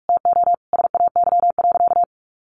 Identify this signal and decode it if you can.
Morse code, heard as a keyed tone, about 44 wpm, 700 Hz. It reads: TO SAY.
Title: Sentences from Top 1200 Words 45wpm